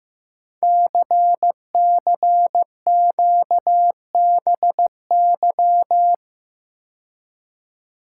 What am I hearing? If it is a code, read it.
Morse code, heard as a keyed tone, about 15 wpm, 700 Hz. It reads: CCQBY
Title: Code Groups - 5 Letters - 15wpm